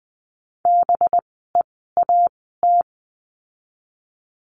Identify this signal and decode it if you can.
Morse code, heard as a keyed tone, about 20 wpm, spaced as for 14 wpm, 700 Hz. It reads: BEAT
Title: 185 Lesson 30f - Letter b - Words 20wpm